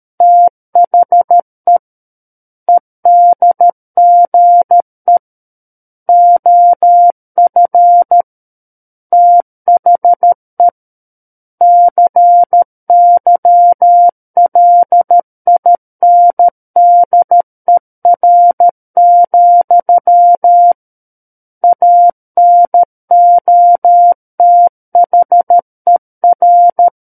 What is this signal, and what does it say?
Morse code, heard as a keyed tone, about 13 wpm, 700 Hz. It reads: THE EDGE OF THE CYLINDER, ANOTHER
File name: Book One / 04-Chapter 4 - 13 WPM